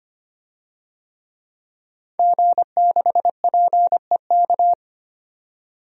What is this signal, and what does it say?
Morse code, heard as a keyed tone, about 25 wpm, 700 Hz. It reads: G6PEK
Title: DXCC-100 Call Signs - 5 Characters - 25wpm